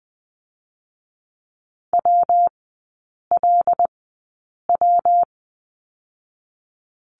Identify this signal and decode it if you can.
Morse code, heard as a keyed tone, about 20 wpm, spaced as for 8 wpm, 690 Hz. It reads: WLW
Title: 205 Lesson 33e - Letter j - Three Characters 20wpm